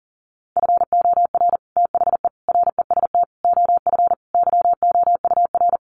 Morse code, heard as a keyed tone, about 40 wpm, 700 Hz. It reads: FOR THE REST OF YOUR
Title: Sets of 5 Words Encore 40wpm